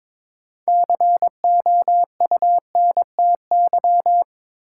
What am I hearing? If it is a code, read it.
Morse code, heard as a keyed tone, about 22 wpm, 700 Hz. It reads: COUNTY
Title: Sentences from Top 1200 Words - New Word Drill - 22wpm